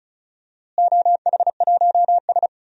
Morse code, heard as a keyed tone, about 36 wpm, 700 Hz. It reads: OH1S
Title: DXCC-100 Call Signs - 4 Characters - Repeated 3x - 35wpm